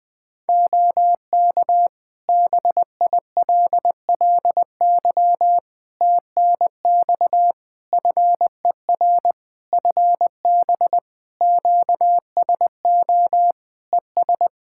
Morse code, heard as a keyed tone, about 20 wpm, 700 Hz. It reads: OK BILLY TNX FER FB QSO ES
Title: Practice QSO - Exchange 4 - Ending - 20wpm